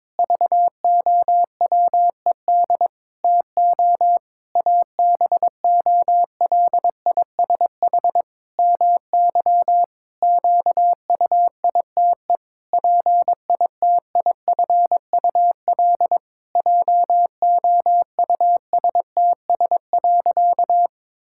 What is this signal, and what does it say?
Morse code, heard as a keyed tone, about 22 wpm, 700 Hz. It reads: VOWED TO ABOLISH MY QUITE PITIFUL JOUSTS.
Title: Pangram Challenge v2 - 22wpm